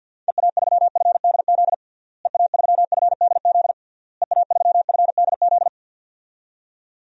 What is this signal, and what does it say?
Morse code, heard as a keyed tone, about 52 wpm, 700 Hz. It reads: EA3FDZ EA3FDZ EA3FDZ
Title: DXCC-100 Call Signs - 6 Characters - Repeated 3x - 50wpm